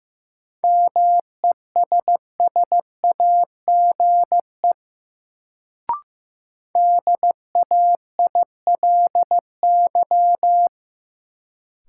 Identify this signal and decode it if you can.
Morse code, heard as a keyed tone, about 15 wpm, 700 Hz. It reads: MESSAGE DAILY
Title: Sentences from Top 1100 Words - New Word Drill - 15wpm